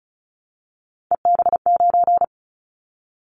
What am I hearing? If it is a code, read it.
Morse code, heard as a keyed tone, about 36 wpm, 700 Hz. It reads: EB9
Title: Callsign Prefixes for Top 100 DXCC - 35wpm